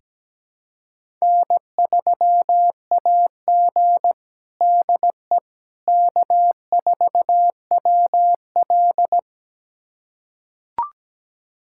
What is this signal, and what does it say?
Morse code, heard as a keyed tone, about 17 wpm, 700 Hz. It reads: N3AG DE K4WL
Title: USA Callsign DE Callsign - 4 Characters - 17wpm